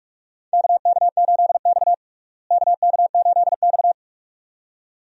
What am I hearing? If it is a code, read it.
Morse code, heard as a keyed tone, about 42 wpm, 700 Hz. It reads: KK8X KK8X
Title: USA Call Signs - 4 Characters - Repeated 2x - 45wpm